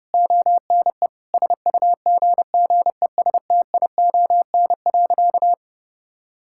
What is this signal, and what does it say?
Morse code, heard as a keyed tone, about 30 wpm, 700 Hz. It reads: ONE SUGGESTION.
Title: Sentences from Top 1600 Words 30wpm